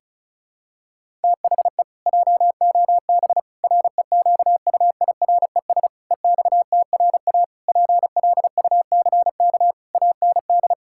Morse code, heard as a keyed tone, about 34 wpm, 700 Hz. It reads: THE JOB REQUIRES EXTRA PLUCK AND
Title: Pangram Challenge v2 - 35wpm